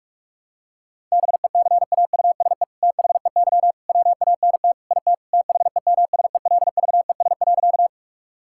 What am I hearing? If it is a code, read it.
Morse code, heard as a keyed tone, about 46 wpm, 700 Hz. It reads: BECAUSE THEY WANT IT THEMSELVES.
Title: Sentences from Top 1000 Words - Review Part 5 - 45wpm